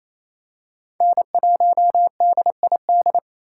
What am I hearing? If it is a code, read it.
Morse code, heard as a keyed tone, about 28 wpm, 700 Hz. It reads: N1DID
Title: K1USN SST QSO Practice - 28wpm